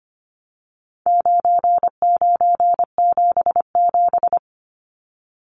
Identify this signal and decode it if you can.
Morse code, heard as a keyed tone, about 25 wpm, 700 Hz. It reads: 9977
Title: Numbers - 4-Digit 25wpm